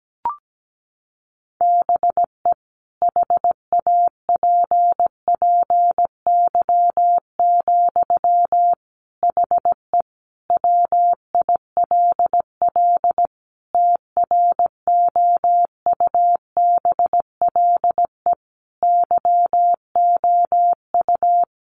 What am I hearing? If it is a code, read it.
Morse code, heard as a keyed tone, about 17 wpm, 700 Hz. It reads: BE HAPPY, HE WILL TROUBLE YOU
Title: Sentences from Top 1200 Words 17wpm